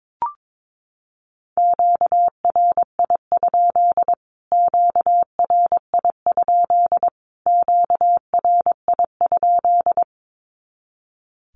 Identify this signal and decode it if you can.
Morse code, heard as a keyed tone, about 22 wpm, 700 Hz. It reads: QRI? QRI? QRI?
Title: Q-Codes Repeated 3x 22wpm